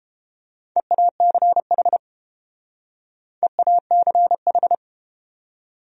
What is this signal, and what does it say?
Morse code, heard as a keyed tone, about 30 wpm, 700 Hz. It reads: EACH EACH